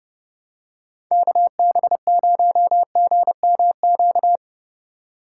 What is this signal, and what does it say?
Morse code, heard as a keyed tone, about 29 wpm, 700 Hz. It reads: KB0GMQ
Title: USA Call Signs - 6 Characters - 8x Character Spacing - 30wpm